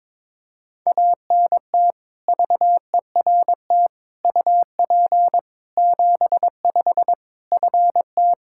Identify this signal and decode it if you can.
Morse code, heard as a keyed tone, about 22 wpm, 700 Hz. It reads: ANT VERT UP 75 FT
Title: Practice QSO - Exchange 2 - Info - 22wpm